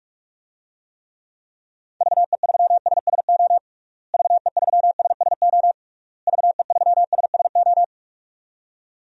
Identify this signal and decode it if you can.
Morse code, heard as a keyed tone, about 46 wpm, 700 Hz. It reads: VE3SSO VE3SSO VE3SSO
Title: DXCC-100 Call Signs - 6 Characters - Repeated 3x - 45wpm